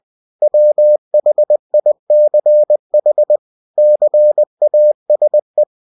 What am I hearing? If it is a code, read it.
Morse code, heard as a keyed tone, about 20 wpm, 600 Hz. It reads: WHICH CASE
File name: cw-books / winnie-the-pooh / winnie-20wpm-page-0062